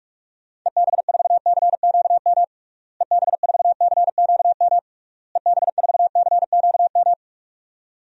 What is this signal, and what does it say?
Morse code, heard as a keyed tone, about 46 wpm, 700 Hz. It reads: EB4CQM EB4CQM EB4CQM
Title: DXCC-100 Call Signs - 6 Characters - Repeated 3x - 45wpm